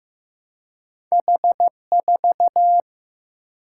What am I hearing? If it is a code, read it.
Morse code, heard as a keyed tone, about 15 wpm, 700 Hz. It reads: H 4
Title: Single-Letter Single-Number 15wpm